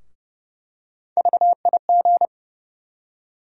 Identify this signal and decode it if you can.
Morse code, heard as a keyed tone, about 30 wpm, 700 Hz. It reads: VIG
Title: USA Call Sign Postfixes - 3 Letters 30wpm